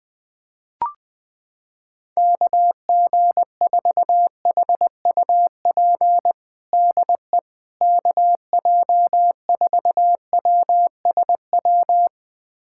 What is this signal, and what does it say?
Morse code, heard as a keyed tone, about 21 wpm, 700 Hz. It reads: KG4HUP DE KJ4WSW
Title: USA Callsign DE Callsign - 6 Characters - 20wpm